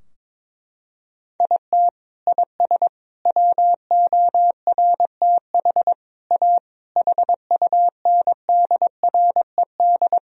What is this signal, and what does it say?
Morse code, heard as a keyed tone, about 21 wpm, 700 Hz. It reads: IT IS WORTH A HUNDRED
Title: Sentences from Top 1000 Words - Review Part 3 - 22wpm